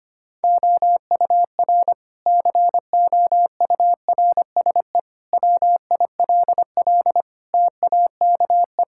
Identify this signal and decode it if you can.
Morse code, heard as a keyed tone, about 25 wpm, 700 Hz. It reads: OUR COURSE WILL TAKE
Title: Sentences from Top 1000 Words - Review Part 6 - 25wpm